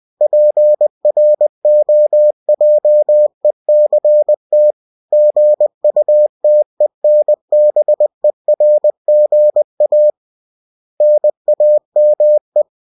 Morse code, heard as a keyed tone, about 20 wpm, 600 Hz. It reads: PROJECT GUTENBERGA NAME
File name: cw-books / winnie-the-pooh / winnie-20wpm-page-0114